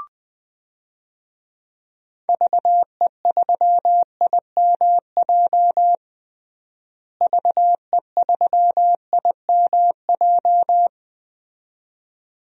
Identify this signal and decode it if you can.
Morse code, heard as a keyed tone, about 20 wpm, 700 Hz. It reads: VE3IMJ VE3IMJ
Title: DXCC-100 Call Signs - 6 Characters - Repeated 2x - 20wpm